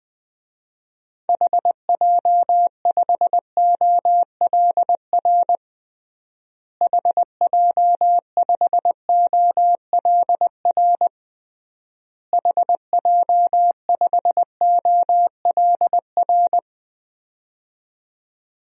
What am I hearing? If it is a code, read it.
Morse code, heard as a keyed tone, about 20 wpm, 700 Hz. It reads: HJ5OLR HJ5OLR HJ5OLR
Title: DXCC-100 Call Signs - 6 Characters - Repeated 3x - 20wpm